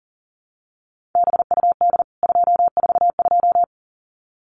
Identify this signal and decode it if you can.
Morse code, heard as a keyed tone, about 40 wpm, 700 Hz. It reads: BUD 242
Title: CWT Exchange 40wpm